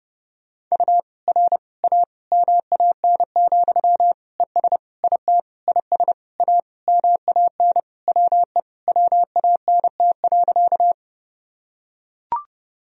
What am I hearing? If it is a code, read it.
Morse code, heard as a keyed tone, about 30 wpm, 700 Hz. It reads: U R A MAN, ES IT IS A MAN WE WANT.
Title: Sentences from Top 200 Words and Abbreviations 30wpm